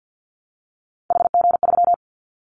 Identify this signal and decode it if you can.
Morse code, heard as a keyed tone, about 49 wpm, 700 Hz. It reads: HG3
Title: Callsign Prefixes for Top 25 DXCC - Repeated 3x - 50wpm